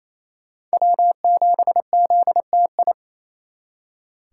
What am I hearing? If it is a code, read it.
Morse code, heard as a keyed tone, about 36 wpm, 700 Hz. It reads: W7ZTI